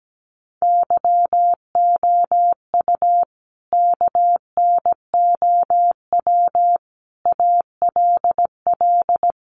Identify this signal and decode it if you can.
Morse code, heard as a keyed tone, about 17 wpm, 700 Hz. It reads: YOU KNOW ALL
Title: Sentences from Top 300 Words 17wpm